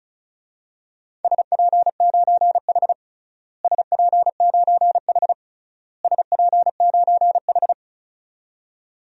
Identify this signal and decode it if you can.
Morse code, heard as a keyed tone, about 34 wpm, 700 Hz. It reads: SP9H SP9H SP9H
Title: DXCC-100 Call Signs - 4 Characters - Repeated 3x - 35wpm